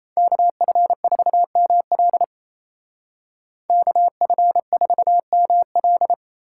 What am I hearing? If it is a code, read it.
Morse code, heard as a keyed tone, about 37 wpm, 700 Hz. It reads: KF4ML KF4ML